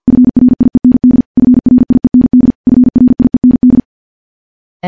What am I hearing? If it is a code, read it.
Morse code, heard as a keyed tone, about 51 wpm, 250 Hz. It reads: FRIEND FRIEND FRIEND
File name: morse / morse-training-50WPM